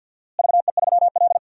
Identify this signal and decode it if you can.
Morse code, heard as a keyed tone, about 52 wpm, 700 Hz. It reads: VE3L
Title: DXCC-100 Call Signs - 4 Characters - Repeated 2x - 50wpm